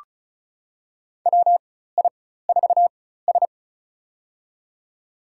Morse code, heard as a keyed tone, about 34 wpm, spaced as for 16 wpm, 700 Hz. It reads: WI4S
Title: USA Call Signs - 4 Characters - 4x Character Spacing - 35wpm